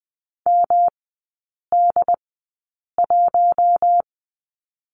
Morse code, heard as a keyed tone, about 20 wpm, spaced as for 8 wpm, 700 Hz. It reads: MD1
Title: 145 Lesson 24e - Letter y - Three Characters 20wpm